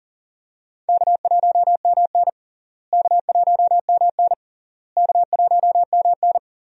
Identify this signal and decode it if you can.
Morse code, heard as a keyed tone, about 40 wpm, 700 Hz. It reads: K1MN K1MN K1MN
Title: USA Call Signs - 4 Characters - Repeated 3x - 40wpm